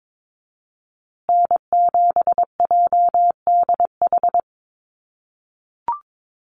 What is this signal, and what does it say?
Morse code, heard as a keyed tone, about 22 wpm, 700 Hz. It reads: N7JDH
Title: USA Call Signs Repeated 2x 22wpm